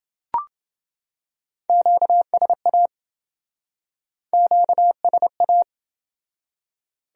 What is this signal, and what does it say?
Morse code, heard as a keyed tone, about 28 wpm, 700 Hz. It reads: QSA QSA